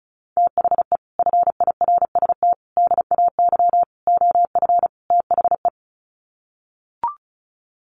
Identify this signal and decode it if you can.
Morse code, heard as a keyed tone, about 34 wpm, 700 Hz. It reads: THE FIRST DAY OF THE
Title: Sets of 5 Words Encore2 35wpm